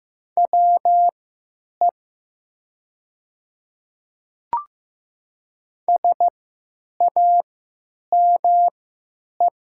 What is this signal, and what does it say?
Morse code, heard as a keyed tone, about 15 wpm, spaced as for 9 wpm, 700 Hz. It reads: WE SAME
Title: 3x Character Spacing - Top 300 Words - 15wpm